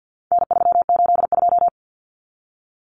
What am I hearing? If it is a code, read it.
Morse code, heard as a keyed tone, about 50 wpm, 700 Hz. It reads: N382
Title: Common Cut Numbers - 4-Digit 50wpm